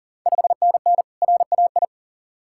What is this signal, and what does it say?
Morse code, heard as a keyed tone, about 40 wpm, 700 Hz. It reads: 5NN RAI
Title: Ontario QSO Party Exchange 40wpm